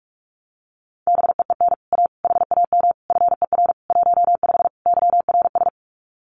Morse code, heard as a keyed tone, about 46 wpm, 700 Hz. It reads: BEEN A HAM FER 15 YRS
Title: Common QSO Phrases Encore 45wpm